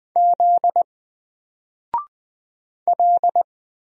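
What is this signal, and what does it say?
Morse code, heard as a keyed tone, about 20 wpm, 700 Hz. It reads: Z L
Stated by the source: Single letter and number Morse drill at 20 wpm, from N4LCD.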